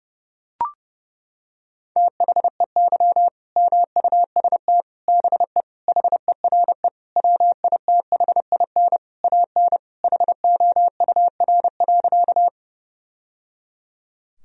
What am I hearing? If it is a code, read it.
Morse code, heard as a keyed tone, about 30 wpm, 700 Hz. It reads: THEY MUST BE HERE WITHIN AN HOUR.
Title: Sentences from Top 1000 Words - Review Part 4 - 30wpm